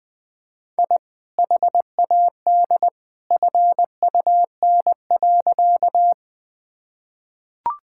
Morse code, heard as a keyed tone, about 21 wpm, 700 Hz. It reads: I HAD FUN.
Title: Sentences with 2-Word Phrases - Spoken Beginning - 20wpm